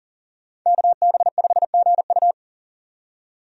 Morse code, heard as a keyed tone, about 40 wpm, 700 Hz. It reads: KB5GU
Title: USA Call Signs - 5 Characters - 40wpm